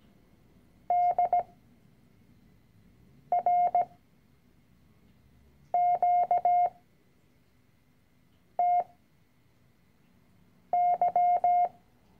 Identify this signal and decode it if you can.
Morse code, heard as a keyed tone, about 17 wpm, spaced as for 4 wpm, 710 Hz. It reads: DRQTY
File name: CW-Online-Kurs / Anfänger / Lektion 010